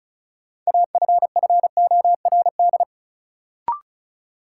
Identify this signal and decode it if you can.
Morse code, heard as a keyed tone, about 36 wpm, 700 Hz. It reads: AFFORD
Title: Sentences from Top 1100 Words - New Word Drill - 35wpm